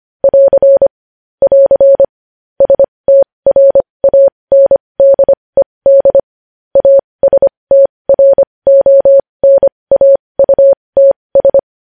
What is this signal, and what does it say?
Morse code, heard as a keyed tone, about 25 wpm, spaced as for 20 wpm, 550 Hz. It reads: <AR> <AR> STRANDED ASTRONAUTS
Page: news-headlines-in-morse-code-at-20-wpm